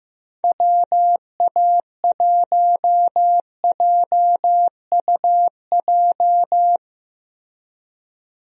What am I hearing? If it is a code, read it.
Morse code, heard as a keyed tone, about 15 wpm, 700 Hz. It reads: WA1JUJ
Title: USA Call Signs - 6 Characters - Repeated 3x - 15wpm